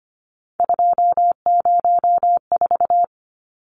Morse code, heard as a keyed tone, about 25 wpm, 700 Hz. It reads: 204
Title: Numbers - 3-Digit 25wpm